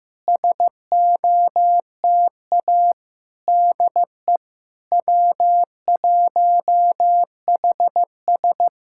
Morse code, heard as a keyed tone, about 15 wpm, 690 Hz. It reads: SOTA DE W1HS
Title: SOTA QSO Practice 15wpm